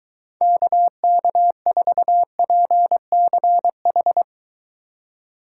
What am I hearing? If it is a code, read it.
Morse code, heard as a keyed tone, about 23 wpm, 700 Hz. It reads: KK4PCH